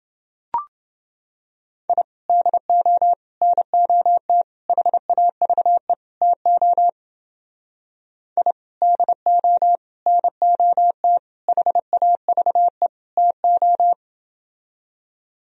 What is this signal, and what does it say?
Morse code, heard as a keyed tone, about 29 wpm, 700 Hz. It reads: I DO NOT HAVE TO I DO NOT HAVE TO